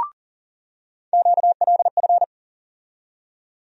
Morse code, heard as a keyed tone, about 40 wpm, 700 Hz. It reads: QLF
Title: USA Call Sign Postfixes - 3 Letters 40wpm